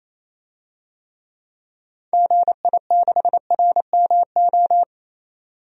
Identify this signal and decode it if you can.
Morse code, heard as a keyed tone, about 28 wpm, 700 Hz. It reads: GI6RMO
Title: DXCC-100 Call Signs - 6 Characters - 28wpm